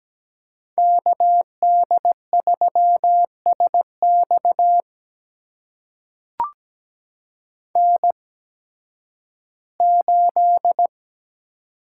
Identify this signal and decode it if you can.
Morse code, heard as a keyed tone, about 17 wpm, spaced as for 11 wpm, 700 Hz. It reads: KD3SX N8
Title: USA Call Signs - 5 Characters - 8x Character Spacing - 17wpm